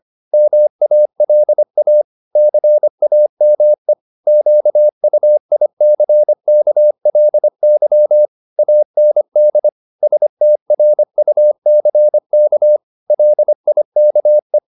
Morse code, heard as a keyed tone, about 25 wpm, 600 Hz. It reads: MALA CAME QUICKLY AND STRUCK LIKE
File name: the old man and the sea part 1 25wpm